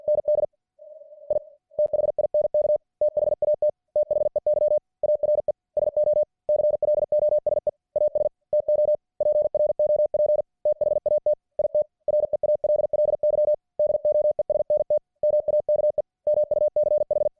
Morse code, heard as a keyed tone, about 53 wpm, 600 Hz. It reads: ND I THINK THAT THEY ARE SO CLOSE AS TO PROP THAT IT REALLY DOESNT MAKE MUCH